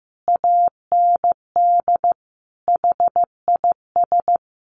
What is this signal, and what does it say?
Morse code, heard as a keyed tone, about 15 wpm, 700 Hz. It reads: AND HIS
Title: Sentences from Top 1400 Words 15wpm